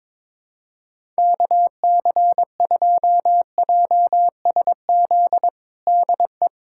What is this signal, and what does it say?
Morse code, heard as a keyed tone, about 22 wpm, 700 Hz. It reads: KC2JSZ DE
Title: USA Callsign DE Callsign - 6 Characters - 22wpm